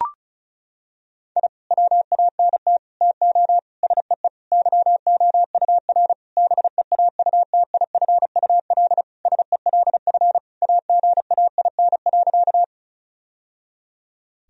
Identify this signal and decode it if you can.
Morse code, heard as a keyed tone, about 36 wpm, 700 Hz. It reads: I WANT TO SEE YOUR BEAUTIFUL SELF AGAIN.
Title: Sentences from Top 1000 Words - Review Part 1 - 35wpm